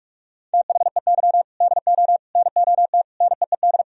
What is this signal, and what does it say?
Morse code, heard as a keyed tone, about 44 wpm, 700 Hz. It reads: THEY DO NOT NEED
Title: Sentences from Top 1000 Words - Review Part 1 - 45wpm